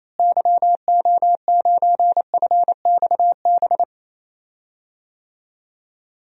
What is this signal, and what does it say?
Morse code, heard as a keyed tone, about 28 wpm, 700 Hz. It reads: YO9FXB
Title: DXCC-100 Call Signs - 6 Characters - Repeated 3x - 28wpm